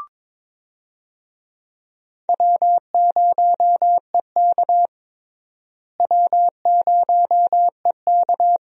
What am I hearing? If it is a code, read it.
Morse code, heard as a keyed tone, about 21 wpm, 700 Hz. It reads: W0EK W0EK
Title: USA Call Signs - 4 Characters - Repeated 3x - 22wpm